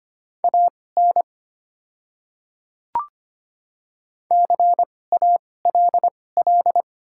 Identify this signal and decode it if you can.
Morse code, heard as a keyed tone, about 25 wpm, spaced as for 18 wpm, 700 Hz. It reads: AN CALL
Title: 2x Character Spacing - Top 100 Words - 25wpm